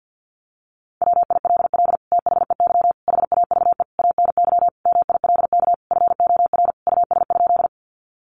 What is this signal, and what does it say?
Morse code, heard as a keyed tone, about 48 wpm, 700 Hz. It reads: WILL THEY HAVE ANY MILK FOR US?
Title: Sentences from Top 1500 Words 50wpm